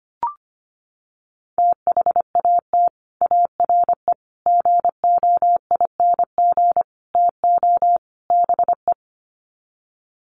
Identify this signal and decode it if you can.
Morse code, heard as a keyed tone, about 25 wpm, 700 Hz. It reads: THAT ARE GOING TO BE
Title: Sets of 5 Words Encore2 25wpm